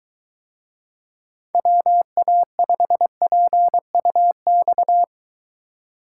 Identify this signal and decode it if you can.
Morse code, heard as a keyed tone, about 23 wpm, 700 Hz. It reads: WA5PUX